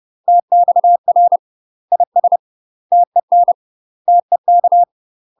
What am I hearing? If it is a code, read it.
Morse code, heard as a keyed tone, about 30 wpm, 700 Hz. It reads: TXR IS TEN TEK
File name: UK MASTER QSOs TRAINING / 30WPM / UK MASTER QSOs 0004